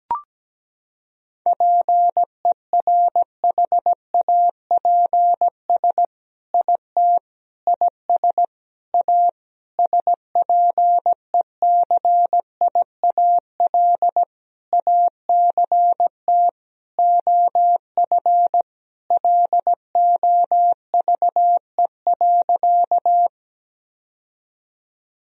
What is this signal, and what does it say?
Morse code, heard as a keyed tone, about 17 wpm, 700 Hz. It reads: PERHAPS IT IS A SPECIAL ACT OF LOVE.
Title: Sentences from Top 1000 Words - Review Part 3 - 17wpm